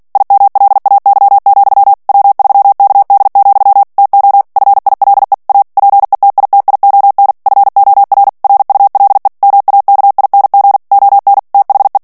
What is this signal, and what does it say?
Morse code, heard as a keyed tone, about 48 wpm, 800 Hz. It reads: IMLAY, W3KD, TO FILE A PETITION FOR RULE MAKING ON THE
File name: ANGLAIS texte clair 3 vitesse 45 mots minute